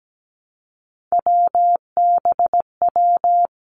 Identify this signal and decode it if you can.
Morse code, heard as a keyed tone, about 17 wpm, 700 Hz. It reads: WBW
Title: Three Challenging Letter-Number Combinations 17wpm